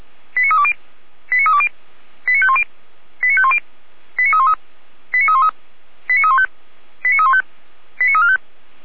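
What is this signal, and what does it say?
Five-tone selective call (ZVEI), heard as quick runs of five tones, tones of about 70 ms. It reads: 89320 89320 89510 89510 89313 89313 89315 89315 89345
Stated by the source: ZVEI five-tone call received by SDR.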